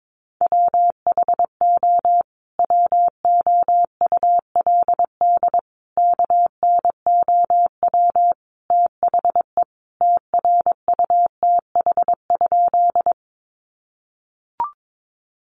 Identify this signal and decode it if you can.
Morse code, heard as a keyed tone, about 22 wpm, 700 Hz. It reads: WHO WOULD KNOW THE TRUTH?
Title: Sentences from Top 800 Words 22wpm